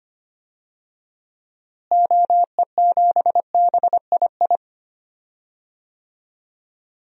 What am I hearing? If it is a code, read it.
Morse code, heard as a keyed tone, about 25 wpm, 700 Hz. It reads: OE7BII
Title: DXCC-100 Call Signs - 6 Characters - Repeated 3x - 25wpm